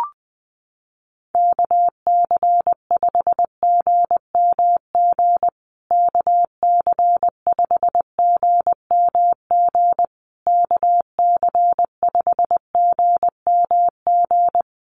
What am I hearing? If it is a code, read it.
Morse code, heard as a keyed tone, about 20 wpm, 700 Hz. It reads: KC5GMG KC5GMG KC5GMG
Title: USA Call Signs Repeated 3x 20wpm